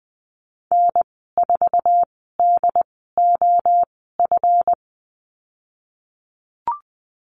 Morse code, heard as a keyed tone, about 20 wpm, spaced as for 15 wpm, 700 Hz. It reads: N4DOF
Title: 173 Lesson 28g - Number 9 - Call Signs 20wpm